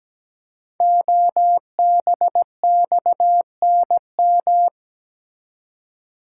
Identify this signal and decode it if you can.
Morse code, heard as a keyed tone, about 17 wpm, 700 Hz. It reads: OBXNM
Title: Code Groups - 5 Letters - 17wpm